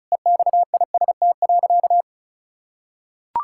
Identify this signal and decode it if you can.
Morse code, heard as a keyed tone, about 34 wpm, 700 Hz. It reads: EXIST.
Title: Sentences from Top 1000 Words - Review Part 5 - 35wpm